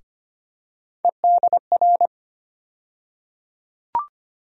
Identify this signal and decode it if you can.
Morse code, heard as a keyed tone, about 25 wpm, 700 Hz. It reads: EDR
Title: USA Call Sign Postfixes - 3 Letters 25wpm